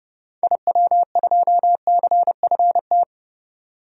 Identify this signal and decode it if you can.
Morse code, heard as a keyed tone, about 30 wpm, 700 Hz. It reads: IW2CFT
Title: DXCC-100 Call Signs - 6 Characters - 30wpm